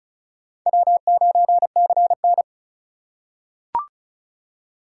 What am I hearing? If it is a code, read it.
Morse code, heard as a keyed tone, about 36 wpm, 700 Hz. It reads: W9CN
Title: USA Call Signs - 4 Characters - 4x Character Spacing - 35wpm